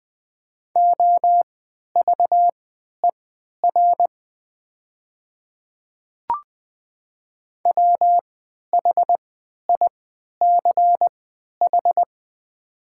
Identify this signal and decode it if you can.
Morse code, heard as a keyed tone, about 20 wpm, spaced as for 11 wpm, 700 Hz. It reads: OVER WHICH
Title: 3x Character Spacing - Top 100 Words - 20wpm